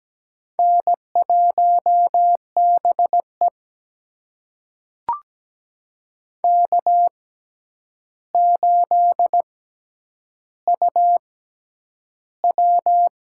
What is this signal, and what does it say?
Morse code, heard as a keyed tone, about 17 wpm, spaced as for 9 wpm, 700 Hz. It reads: N1BE K8UW
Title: USA Call Signs - 4 Characters - 6x Character Spacing - 17wpm